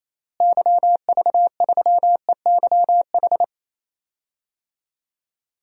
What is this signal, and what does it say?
Morse code, heard as a keyed tone, about 27 wpm, 700 Hz. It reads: YV3EYH
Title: DXCC-100 Call Signs - 6 Characters - Repeated 2x - 28wpm